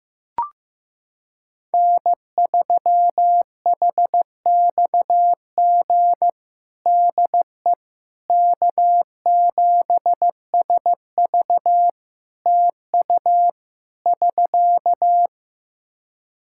Morse code, heard as a keyed tone, about 15 wpm, 700 Hz. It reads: N3HXG DE K7SV TU <SK>
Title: K1USN SST QSO Practice 15wpm